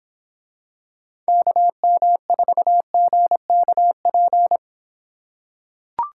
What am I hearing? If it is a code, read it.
Morse code, heard as a keyed tone, about 25 wpm, 700 Hz. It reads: KM4GKP